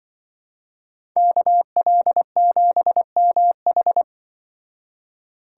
Morse code, heard as a keyed tone, about 24 wpm, 700 Hz. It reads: KL7MH
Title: Speed Racing - USA 2x2 Call Signs 20wpm